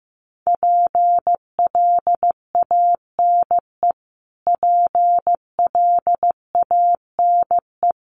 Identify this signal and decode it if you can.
Morse code, heard as a keyed tone, about 15 wpm, 700 Hz. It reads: PLANE PLANE
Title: Top 500 Words Repeated 15wpm